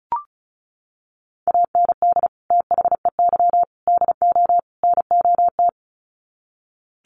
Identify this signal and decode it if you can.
Morse code, heard as a keyed tone, about 34 wpm, 700 Hz. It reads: AND THEY DO NOT
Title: Sets of 4 Words 35wpm